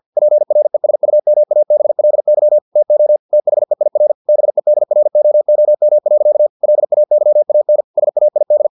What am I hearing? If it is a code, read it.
Morse code, heard as a keyed tone, about 50 wpm, 600 Hz. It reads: PRESUMABLY TO THEIR BEDROOM. LAYAN SAID
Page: Never 49wpm